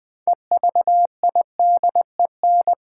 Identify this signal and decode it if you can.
Morse code, heard as a keyed tone, about 20 wpm, 700 Hz. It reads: EVIDEN
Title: Build A Word - 3 Letters At A Time - 8-Letter Words - 20wpm